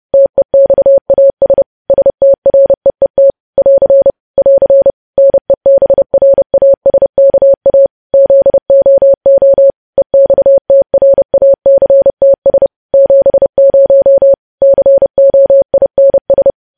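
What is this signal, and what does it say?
Morse code, heard as a keyed tone, about 30 wpm, 550 Hz. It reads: TEXAS STREET <AR> <AR> NEBRASKA ZOO EXTRACTS 70 COINS
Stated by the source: Morse code practice of news headlines sent at 30 wpm.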